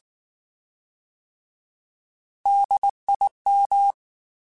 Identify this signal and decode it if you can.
Morse code, heard as a keyed tone, about 19 wpm, 800 Hz. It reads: DIM
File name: Alphabetical / Alphabetical mp3s / Ds @18